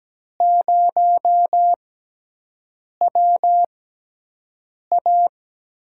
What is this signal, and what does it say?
Morse code, heard as a keyed tone, about 17 wpm, spaced as for 6 wpm, 700 Hz. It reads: 0WA
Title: USA Call Signs - 6 Characters - 6x Character Spacing - 17wpm